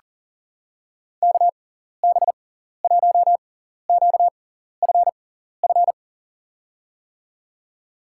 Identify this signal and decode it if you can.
Morse code, heard as a keyed tone, about 40 wpm, spaced as for 14 wpm, 700 Hz. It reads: KB1QFF
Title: USA Call Signs - 6 Characters - 6x Character Spacing - 40wpm